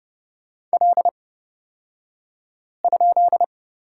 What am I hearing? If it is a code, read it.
Morse code, heard as a keyed tone, about 30 wpm, spaced as for 5 wpm, 700 Hz. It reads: L?
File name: Lesson 37d - Letter x - Two Characters 30wpm